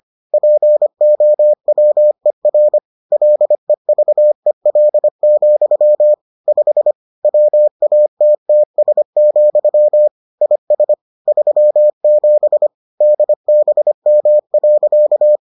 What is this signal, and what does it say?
Morse code, heard as a keyed tone, about 25 wpm, 600 Hz. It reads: POWER LEVEL, 5 WATTS, IS 37 DBM.